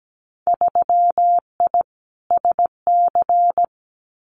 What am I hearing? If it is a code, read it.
Morse code, heard as a keyed tone, about 17 wpm, 700 Hz. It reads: 3I SC
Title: Winter Field Day Exchanges 17wpm